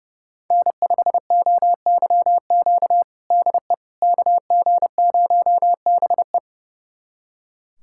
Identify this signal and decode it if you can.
Morse code, heard as a keyed tone, about 30 wpm, 700 Hz. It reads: N5OYQ DE KG0BE
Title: USA Callsign DE Callsign 30wpm